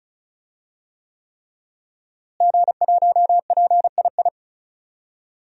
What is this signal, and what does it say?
Morse code, heard as a keyed tone, about 36 wpm, 700 Hz. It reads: G1PII
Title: DXCC-100 Call Signs - 5 Characters - Repeated 3x - 35wpm